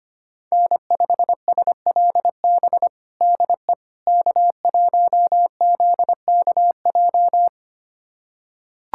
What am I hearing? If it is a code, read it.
Morse code, heard as a keyed tone, about 25 wpm, 700 Hz. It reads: N5SLB DE K1ZKJ
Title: USA Callsign DE Callsign 25wpm